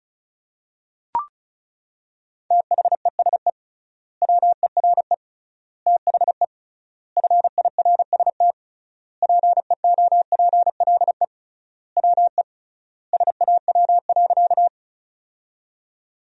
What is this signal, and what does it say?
Morse code, heard as a keyed tone, about 36 wpm, 700 Hz. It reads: THESE WERE THE FIRST PEOPLE WE SAW.
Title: 3x Word-Spacing - Sentences from Top 300 Words - 35wpm